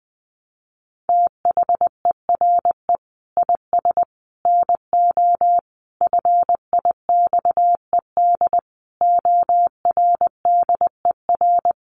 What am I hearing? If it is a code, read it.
Morse code, heard as a keyed tone, about 20 wpm, 700 Hz. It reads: THERE IS NO FIXED ORDER
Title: Sentences from Top 1100 Words 20wpm